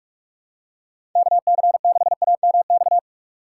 Morse code, heard as a keyed tone, about 43 wpm, 700 Hz. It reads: KC6AMX
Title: USA Call Signs - 6 Characters - 4x Character Spacing - 45wpm